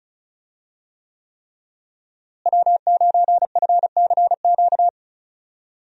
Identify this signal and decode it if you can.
Morse code, heard as a keyed tone, about 36 wpm, 700 Hz. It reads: W9FCQ
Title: USA Call Signs - 5 Characters - Repeated 3x - 35wpm